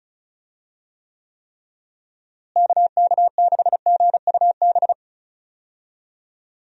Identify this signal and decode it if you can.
Morse code, heard as a keyed tone, about 34 wpm, 700 Hz. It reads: KK6GUB
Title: USA Call Signs - 6 Characters - Repeated 3x - 35wpm